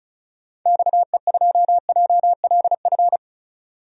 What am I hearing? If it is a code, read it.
Morse code, heard as a keyed tone, about 36 wpm, 700 Hz. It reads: XE2JLF
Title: DXCC-100 Call Signs - 6 Characters - Repeated 3x - 35wpm